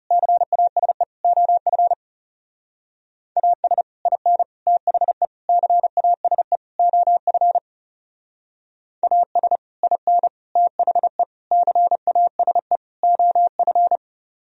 Morse code, heard as a keyed tone, about 38 wpm, 700 Hz. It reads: CASE OF AS IN THE CASE OF AS IN THE CASE OF
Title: Speed-Racing - Sets of 5 Words 30wpm